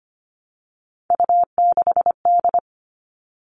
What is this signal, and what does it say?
Morse code, heard as a keyed tone, about 25 wpm, 700 Hz. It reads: U6D
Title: All Cut Numbers - 3-Digit 25wpm